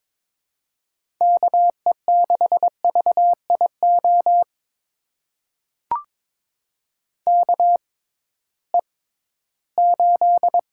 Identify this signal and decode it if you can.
Morse code, heard as a keyed tone, about 22 wpm, spaced as for 14 wpm, 700 Hz. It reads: KE6VIO KE8
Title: USA Call Signs - 6 Characters - 6x Character Spacing - 22wpm